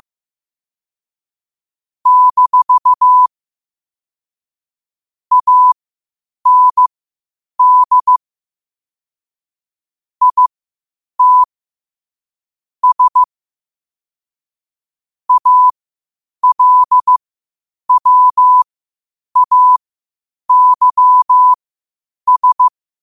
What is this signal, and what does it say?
Morse code, heard as a keyed tone, about 15 wpm, spaced as for 8 wpm, 1000 Hz. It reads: - AND IT S ALWAYS
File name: qotdfiles / 8wpm